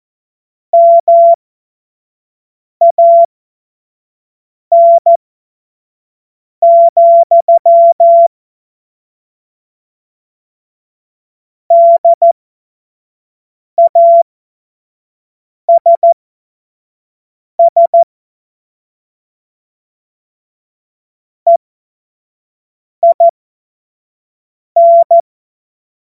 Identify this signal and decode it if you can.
Morse code, heard as a keyed tone, about 14 wpm, spaced as for 5 wpm, 680 Hz. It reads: MAN, DASS EIN